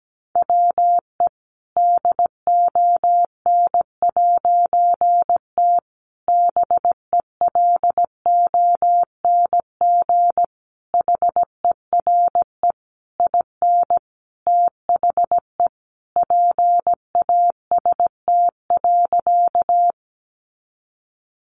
Morse code, heard as a keyed tone, about 17 wpm, 700 Hz. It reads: WE DON'T BELONG HERE IN THE PAST.
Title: A Sound Of Thunder 17wpm